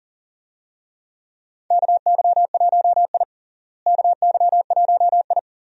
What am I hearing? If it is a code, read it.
Morse code, heard as a keyed tone, about 40 wpm, 700 Hz. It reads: KY1I KY1I
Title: USA Call Signs - 4 Characters - Repeated 2x - 40wpm